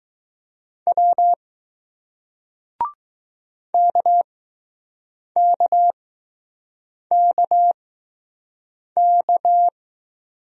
Morse code, heard as a keyed tone, about 20 wpm, spaced as for 6 wpm, 700 Hz. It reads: W KKKK